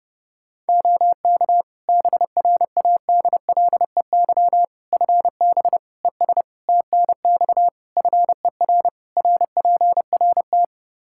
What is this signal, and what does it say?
Morse code, heard as a keyed tone, about 30 wpm, 700 Hz. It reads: OK BRADLEY FB ES TNX FER RPRT
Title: Practice QSO - Exchange 2 - Info - 30wpm